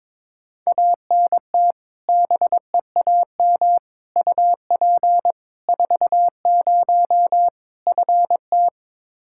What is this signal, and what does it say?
Morse code, heard as a keyed tone, about 22 wpm, 700 Hz. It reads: ANT BEAM UP 40 FT
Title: Practice QSO - Exchange 2 - Info - 22wpm